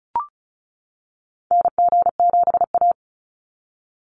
Morse code, heard as a keyed tone, about 36 wpm, 700 Hz. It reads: NG7A
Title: NA SOTA Activator Callsigns 35wpm